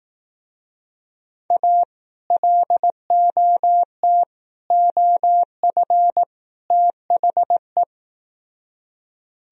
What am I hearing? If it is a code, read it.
Morse code, heard as a keyed tone, about 18 wpm, 700 Hz. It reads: A LOT OF THE
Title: Speed-Racing - Sets of 4 Words 15wpm